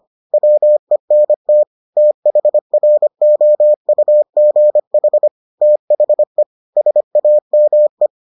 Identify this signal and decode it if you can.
Morse code, heard as a keyed tone, about 25 wpm, 600 Hz. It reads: WENT THROUGH THE SAME